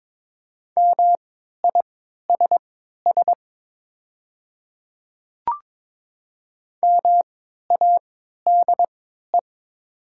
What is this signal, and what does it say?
Morse code, heard as a keyed tone, about 22 wpm, spaced as for 13 wpm, 700 Hz. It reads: MISS MADE
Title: 3x Character Spacing - Top 500 Words - 22wpm